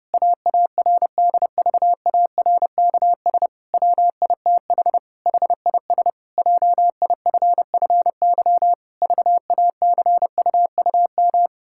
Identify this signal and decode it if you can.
Morse code, heard as a keyed tone, about 31 wpm, 700 Hz. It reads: AARDVARKS WITH HIS JIFFY VACUUM
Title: Pangram Challenge v2 - 30wpm